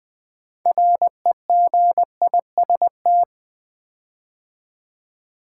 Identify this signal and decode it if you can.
Morse code, heard as a keyed tone, about 20 wpm, 700 Hz. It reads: REGIST
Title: Build A Word - 6 Letters At A Time - 12-Letter Words - 20wpm